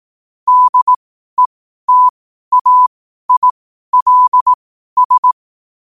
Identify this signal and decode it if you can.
Morse code, heard as a keyed tone, about 18 wpm, spaced as for 12 wpm, 1000 Hz. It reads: DETAILS
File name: qotdfiles / 12wpm